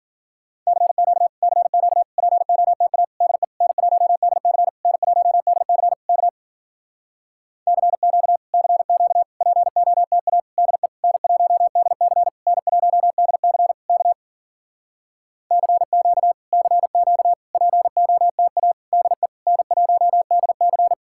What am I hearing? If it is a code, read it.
Morse code, heard as a keyed tone, about 52 wpm, 700 Hz. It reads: CQ CQ POTA DE N1DC N1DC K CQ CQ POTA DE N1DC N1DC K CQ CQ POTA DE N1DC